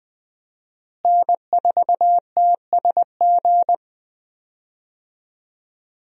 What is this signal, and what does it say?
Morse code, heard as a keyed tone, about 20 wpm, 700 Hz. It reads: N4TSG
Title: USA Call Signs - 5 Characters - Repeated 3x - 20wpm